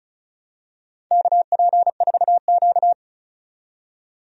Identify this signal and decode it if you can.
Morse code, heard as a keyed tone, about 36 wpm, 700 Hz. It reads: KP4Q
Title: DXCC-100 Call Signs - 4 Characters - 35wpm